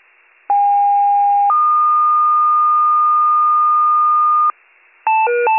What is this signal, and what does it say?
Two-tone page, heard as a one-second tone, then a three-second tone, about 810 then 1220 Hz. It is heard from 0.5 to 4.5 s.